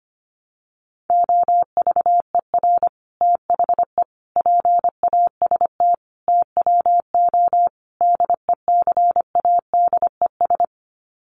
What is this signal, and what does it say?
Morse code, heard as a keyed tone, about 25 wpm, 700 Hz. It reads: OVER THE PAST TWO DECADES
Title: Sets of 5 Words Encore2 25wpm